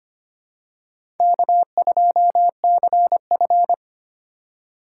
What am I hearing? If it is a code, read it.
Morse code, heard as a keyed tone, about 25 wpm, 700 Hz. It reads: K2CF
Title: USA Call Signs - 4 Characters - 4x Character Spacing - 25wpm